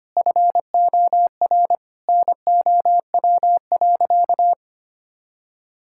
Morse code, heard as a keyed tone, about 25 wpm, 700 Hz. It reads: FOR NOW.
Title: Sentences from Top 1000 Words - Review Part 6 - 25wpm